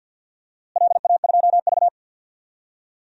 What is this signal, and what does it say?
Morse code, heard as a keyed tone, about 52 wpm, 700 Hz. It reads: LA2V
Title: DXCC-100 Call Signs - 4 Characters - Repeated 3x - 50wpm